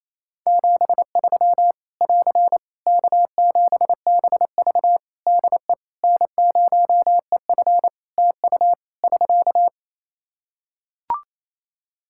Morse code, heard as a keyed tone, about 28 wpm, 700 Hz. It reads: 73 <AR> K7BV DE N0EF TU <SK>
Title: Practice QSO - Exchange 4 - Ending - 28wpm